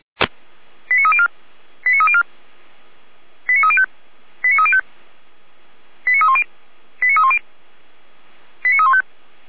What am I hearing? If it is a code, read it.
Five-tone selective call (ZVEI), heard as quick runs of five tones, tones of about 70 ms. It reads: 89304 89304 89305 89305 89310 89310 89315